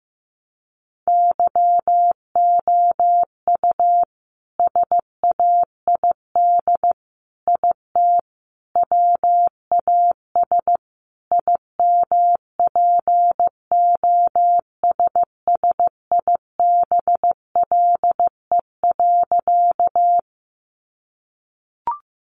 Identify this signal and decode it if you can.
Morse code, heard as a keyed tone, about 15 wpm, 700 Hz. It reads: YOU SAID IT WAS IMPOSSIBLE.
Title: Sentences from Top 1100 Words 15wpm